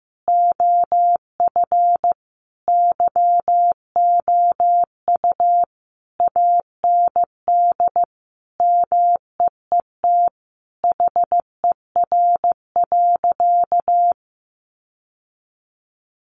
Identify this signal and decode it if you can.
Morse code, heard as a keyed tone, about 15 wpm, 700 Hz. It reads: OF YOU AND MEET HER.
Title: Sentences from Top 1100 Words 15wpm